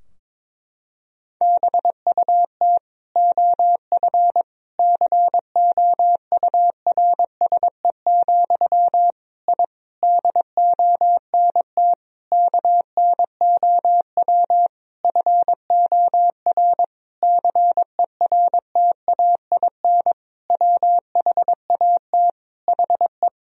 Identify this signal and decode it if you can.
Morse code, heard as a keyed tone, about 22 wpm, 700 Hz. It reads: BUT OF COURSE, I DONT KNOW FOR CERTAIN WHAT HE
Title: Sentences from Top 1000 Words - Review Part 2 - 22wpm